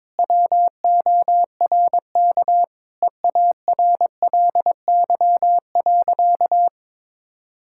Morse code, heard as a keyed tone, about 22 wpm, 700 Hz. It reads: WORK EARLY.
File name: practice / 22wpm / Sentences with 2-Word Phrases - 22wpm